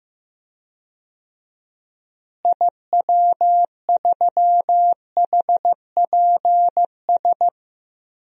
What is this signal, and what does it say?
Morse code, heard as a keyed tone, about 15 wpm, 700 Hz. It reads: IW3HPS
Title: DXCC-100 Call Signs - 6 Characters - Repeated 2x - 15wpm